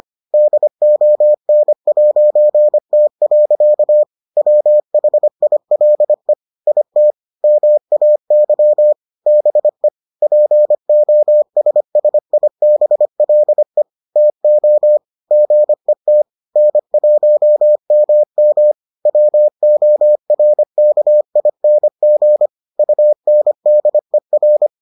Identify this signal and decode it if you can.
Morse code, heard as a keyed tone, about 25 wpm, 600 Hz. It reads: DON'T. WHILE IT MAY BE POSSIBLE TO GET N1MM WORKING UNDER